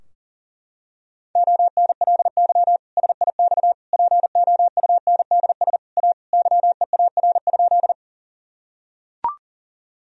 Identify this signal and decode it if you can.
Morse code, heard as a keyed tone, about 37 wpm, 700 Hz. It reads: ONLY SIX POUNDS A YEAR?
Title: Sentences from Top 1000 Words - Review Part 2 - 40wpm